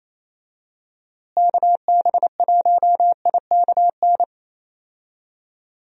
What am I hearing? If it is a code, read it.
Morse code, heard as a keyed tone, about 28 wpm, 700 Hz. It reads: KB1IKN
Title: USA Call Signs - 6 Characters - Repeated 2x - 28wpm